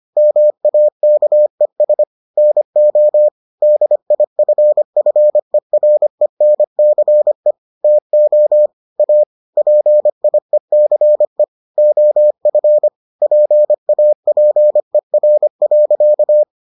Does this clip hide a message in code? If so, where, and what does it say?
Yes: Morse code from 0.2 to 16.4 s, heard as a keyed tone, about 25 wpm, 600 Hz. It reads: MAKES NO DIFFERENCE TO A PIECE OF PAPER.